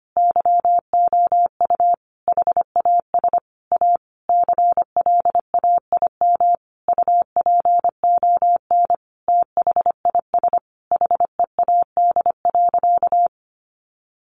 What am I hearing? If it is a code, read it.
Morse code, heard as a keyed tone, about 25 wpm, 700 Hz. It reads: YOU HAS A CLAIM UPON THIS HEAD.
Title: Sentences from Top 1500 Words 25wpm